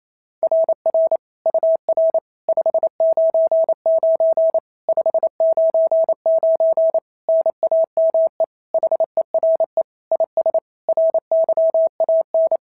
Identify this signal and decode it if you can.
Morse code, heard as a keyed tone, about 25 wpm, 650 Hz. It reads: RR UR 599 599 NAME HERE IS RYAN
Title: SOTA QSO Practice 28wpm